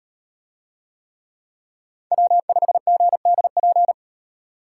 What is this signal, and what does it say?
Morse code, heard as a keyed tone, about 38 wpm, 700 Hz. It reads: W5GDP